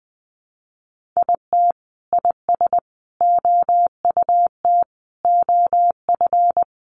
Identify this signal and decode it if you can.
Morse code, heard as a keyed tone, about 19 wpm, 700 Hz. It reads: IT IS OUT OF
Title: Sentences from Top 1400 Words 20wpm